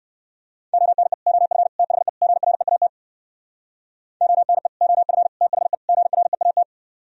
Morse code, heard as a keyed tone, about 58 wpm, 700 Hz. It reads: ONE OF THE GREAT ONE OF THE GREAT